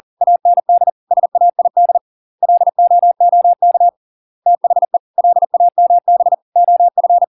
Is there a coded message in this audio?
Yes: Morse code from 0.2 to 7.2 s, heard as a keyed tone, about 40 wpm, 700 Hz. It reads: AND SAID LOOK THE LAMB OF